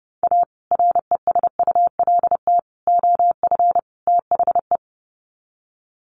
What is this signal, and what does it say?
Morse code, heard as a keyed tone, about 30 wpm, 700 Hz. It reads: A RESULT OF THE
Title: Sets of 4 Words 30wpm